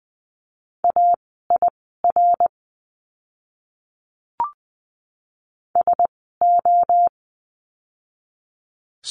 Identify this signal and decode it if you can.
Morse code, heard as a keyed tone, about 20 wpm, spaced as for 14 wpm, 700 Hz. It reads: AIR SO
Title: 061 Lesson 11f - Letter h - Words 20wpm